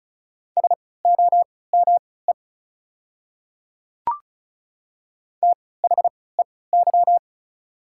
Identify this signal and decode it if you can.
Morse code, heard as a keyed tone, about 36 wpm, spaced as for 20 wpm, 700 Hz. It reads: SOME THEY
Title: 3x Character Spacing - Top 100 Words - 35wpm